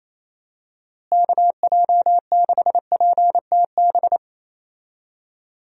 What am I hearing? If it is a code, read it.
Morse code, heard as a keyed tone, about 28 wpm, 700 Hz. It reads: KJ6PTB
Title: USA Call Signs - 6 Characters - Repeated 3x - 28wpm